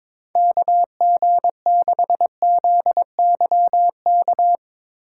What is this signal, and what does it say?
Morse code, heard as a keyed tone, about 21 wpm, 700 Hz. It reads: KG6ZYK
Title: USA Call Signs - 6 Characters - 2x Character Spacing - 22wpm